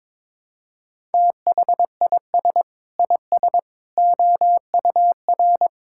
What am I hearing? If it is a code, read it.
Morse code, heard as a keyed tone, about 21 wpm, 700 Hz. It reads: THIS IS OUR
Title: Sentences from Top 1000 Words - Review Part 5 - 22wpm